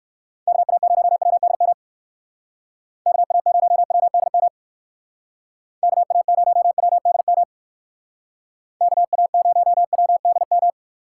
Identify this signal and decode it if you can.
Morse code, heard as a keyed tone, about 58 wpm, 700 Hz. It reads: KA0WDM KA0WDM KA0WDM KA0WDM